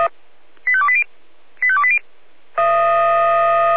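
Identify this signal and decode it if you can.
Five-tone selective call (ZVEI), heard as quick runs of five tones, tones of about 70 ms. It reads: 85280 85280
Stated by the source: Received on SDR by feuerwehr-orlamuende.org.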